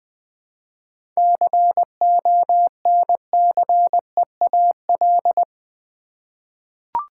Morse code, heard as a keyed tone, about 20 wpm, 700 Hz. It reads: CONCEAL
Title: 7 Letter Words - First 3 Letters Given - 20wpm